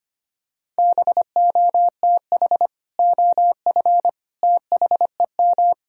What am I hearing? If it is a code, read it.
Morse code, heard as a keyed tone, about 25 wpm, 700 Hz. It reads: BOTH OF THEM
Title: Sentences from Top 1000 Words - Review Part 1 - 25wpm